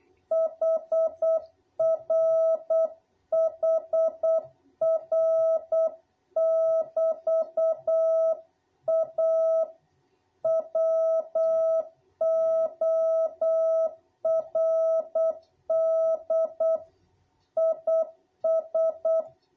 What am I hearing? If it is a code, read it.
Morse code, heard as a keyed tone, about 8 wpm, 660 Hz. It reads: HR HR<BT>A WORD IS
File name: jh8cbh / 2021cw